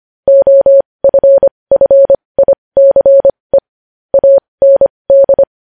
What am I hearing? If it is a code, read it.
Morse code, heard as a keyed tone, about 25 wpm, spaced as for 20 wpm, 550 Hz. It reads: OFFICE AND
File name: podcast / News headlines-1x-25x20